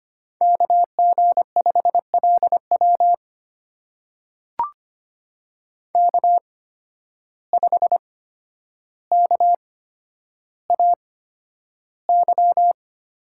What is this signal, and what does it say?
Morse code, heard as a keyed tone, about 25 wpm, spaced as for 11 wpm, 700 Hz. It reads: KG5LW K5KAY
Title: USA Call Signs - 5 Characters - 8x Character Spacing - 25wpm